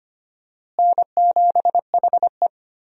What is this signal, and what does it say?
Morse code, heard as a keyed tone, about 25 wpm, 700 Hz. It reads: N7HE
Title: USA Call Signs - 4 Characters - Repeated 3x - 25wpm